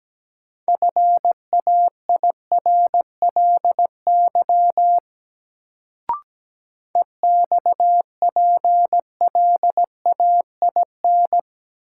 Morse code, heard as a keyed tone, about 17 wpm, 700 Hz. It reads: FAIRLY EXPLAIN
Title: Sentences from Top 1100 Words - New Word Drill - 17wpm